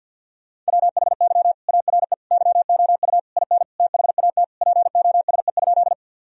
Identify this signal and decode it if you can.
Morse code, heard as a keyed tone, about 52 wpm, 700 Hz. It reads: WHY ARE YOU IN THAT POSE?
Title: Sentences from Top 1000 Words - Review Part 2 - 50wpm